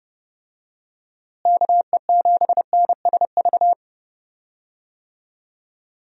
Morse code, heard as a keyed tone, about 30 wpm, 700 Hz. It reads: KE7NSV
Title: USA Call Signs - 6 Characters - Repeated 2x - 30wpm